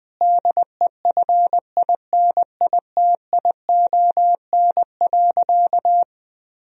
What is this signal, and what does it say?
Morse code, heard as a keyed tone, about 20 wpm, 700 Hz. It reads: DEFINITION.
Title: Sentences from Top 1000 Words - Review Part 4 - 20wpm